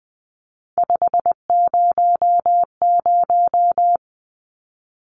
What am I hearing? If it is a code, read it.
Morse code, heard as a keyed tone, about 20 wpm, 700 Hz. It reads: 500
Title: Numbers - 3-Digit 20wpm